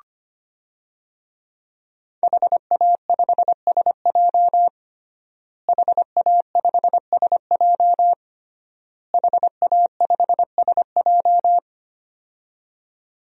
Morse code, heard as a keyed tone, about 25 wpm, 700 Hz. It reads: HA5SJ HA5SJ HA5SJ
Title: DXCC-100 Call Signs - 5 Characters - Repeated 3x - 25wpm